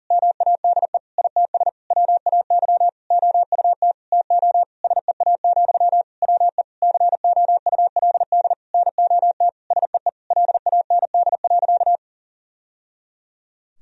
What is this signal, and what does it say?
Morse code, heard as a keyed tone, about 40 wpm, 700 Hz. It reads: MADE ITS WAY OUT TO SEA, WE COULD NOT SEE LAND.
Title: Sentences from Top 1000 Words - Review Part 2 - 40wpm